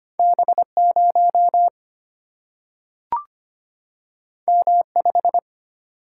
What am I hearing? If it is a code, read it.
Morse code, heard as a keyed tone, about 25 wpm, 700 Hz. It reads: B0 M5
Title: Single-Letter Single-Number 25wpm